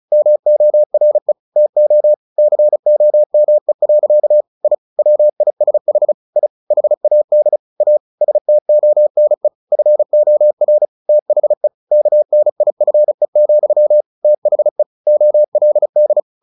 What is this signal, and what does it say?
Morse code, heard as a keyed tone, about 34 wpm, 600 Hz. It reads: MORE TO COME. I WISH I HAD A STONE FOR THE KNIFE, THE OLD
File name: the old man and the sea 2 35wpm